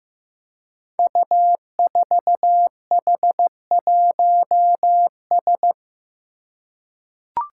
Morse code, heard as a keyed tone, about 15 wpm, 700 Hz. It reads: U4H1S
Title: Five Challenging Letter-Number Combinations 15wpm